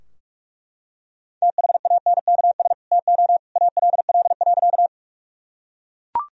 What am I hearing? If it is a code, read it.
Morse code, heard as a keyed tone, about 41 wpm, 700 Hz. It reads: THANKS TO ALL.
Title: Sentences from Top 1000 Words - Review Part 3 - 45wpm